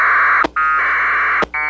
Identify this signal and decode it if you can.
Packet radio (AX.25, as in APRS).
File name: SignalSamples / aprs Hz